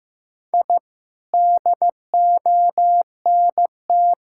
Morse code, heard as a keyed tone, about 15 wpm, 700 Hz. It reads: I DONT
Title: Sentences from Top 1000 Words - Review Part 5 - 15wpm